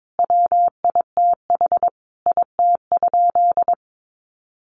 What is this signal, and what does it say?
Morse code, heard as a keyed tone, about 22 wpm, 700 Hz. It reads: WITH IT?
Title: Sentences from Top 100 Words 22wpm